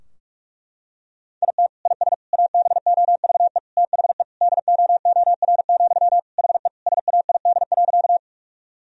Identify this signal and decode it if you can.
Morse code, heard as a keyed tone, about 42 wpm, 700 Hz. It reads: IT IS ABOVE THE DOOR, HE SAID.
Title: Sentences from Top 1000 Words - Review Part 2 - 45wpm